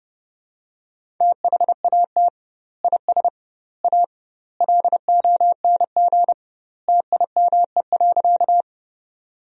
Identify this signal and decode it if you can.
Morse code, heard as a keyed tone, about 30 wpm, 700 Hz. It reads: THAT IS A LONG TIME.
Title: 2x Word-Spacing - Sentences from Top 100 Words - 30wpm